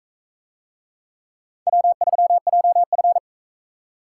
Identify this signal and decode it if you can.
Morse code, heard as a keyed tone, about 42 wpm, 700 Hz. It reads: W3JF